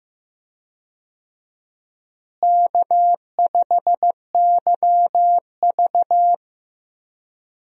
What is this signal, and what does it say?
Morse code, heard as a keyed tone, about 15 wpm, 700 Hz. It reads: K5YV
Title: USA Call Signs - 4 Characters - Repeated 2x - 15wpm